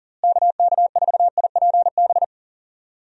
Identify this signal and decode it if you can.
Morse code, heard as a keyed tone, about 40 wpm, 700 Hz. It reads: KK4IPB
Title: USA Call Signs - 6 Characters - Repeated 2x - 40wpm